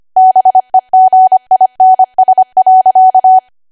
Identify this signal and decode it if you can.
Morse code, heard as a keyed tone, about 25 wpm, 750 Hz. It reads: BEGINS.
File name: Archive / 25 WPM / 25WPM